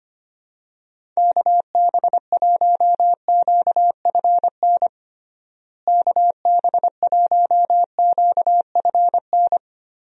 Morse code, heard as a keyed tone, about 25 wpm, 700 Hz. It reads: KB1QFN KB1QFN
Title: USA Call Signs - 6 Characters - Repeated 2x - 25wpm